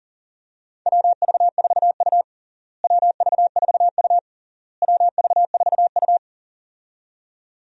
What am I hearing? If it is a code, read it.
Morse code, heard as a keyed tone, about 40 wpm, 700 Hz. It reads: WV4U WV4U WV4U
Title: USA Call Signs - 4 Characters - Repeated 3x - 40wpm